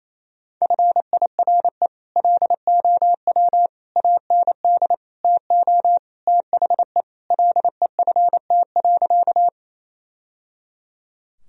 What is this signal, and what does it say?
Morse code, heard as a keyed tone, about 28 wpm, 700 Hz. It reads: FIRE LOW AND TO THE LEFT.
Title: Sentences from Top 1000 Words - Review Part 1 - 28wpm